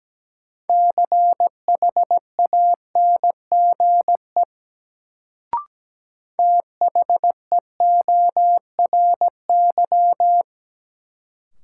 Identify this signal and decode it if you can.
Morse code, heard as a keyed tone, about 17 wpm, 700 Hz. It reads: CHANGE THEORY
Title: Sentences from Top 600 Words - New Word Drill - 17wpm